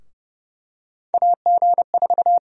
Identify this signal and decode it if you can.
Morse code, heard as a keyed tone, about 30 wpm, 700 Hz. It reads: AG4
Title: USA Call Sign Prefixes - Repeated 2x - 30wpm